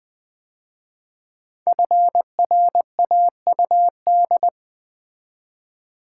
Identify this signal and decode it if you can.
Morse code, heard as a keyed tone, about 20 wpm, 700 Hz. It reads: FRAUD
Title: 5 Letter Words - First 3 Letters Given - 20wpm